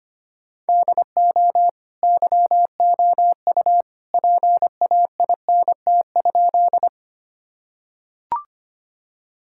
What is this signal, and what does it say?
Morse code, heard as a keyed tone, about 25 wpm, 700 Hz. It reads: DO YOU PAINT?
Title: Sentences from Top 1000 Words - Review Part 6 - 25wpm